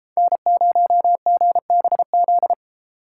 Morse code, heard as a keyed tone, about 33 wpm, 700 Hz. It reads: N0GBZ